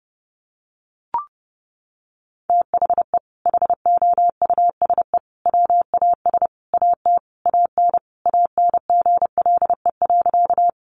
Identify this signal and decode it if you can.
Morse code, heard as a keyed tone, about 30 wpm, 700 Hz. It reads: THE HOUSE WAS AT AN ANGLE.
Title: Sentences from Top 1500 Words 30wpm